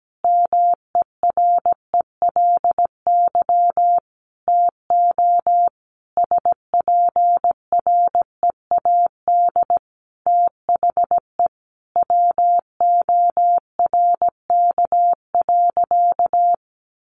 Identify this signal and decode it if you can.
Morse code, heard as a keyed tone, about 17 wpm, 700 Hz. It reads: MERELY TO SPREAD THE WORK.
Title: Sentences from Top 1400 Words 17wpm